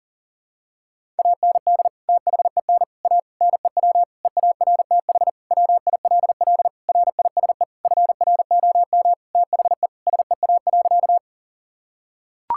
Morse code, heard as a keyed tone, about 40 wpm, 700 Hz. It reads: AND THEN A NEW EARTH WILL RISE FROM THE SEA.
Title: Sentences from Top 1000 Words - Review Part 6 - 40wpm